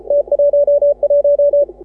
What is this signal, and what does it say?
Morse code, heard as a keyed tone, about 36 wpm, 530 Hz. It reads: T11
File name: AUDIO2018 / HA6OI CW